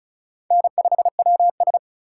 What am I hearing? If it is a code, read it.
Morse code, heard as a keyed tone, about 36 wpm, 700 Hz. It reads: N5WS
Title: USA Call Signs - 4 Characters - 2x Character Spacing - 35wpm